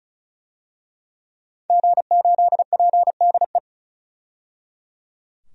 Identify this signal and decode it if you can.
Morse code, heard as a keyed tone, about 34 wpm, 700 Hz. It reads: G8PDE
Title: DXCC-100 Call Signs - 5 Characters - 35wpm